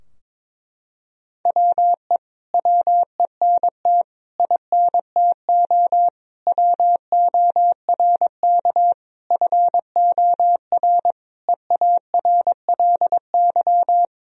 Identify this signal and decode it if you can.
Morse code, heard as a keyed tone, about 22 wpm, 700 Hz. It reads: WE WENT INTO WORK FOR EARLY
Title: Sentences from Top 1000 Words - Review Part 4 - 22wpm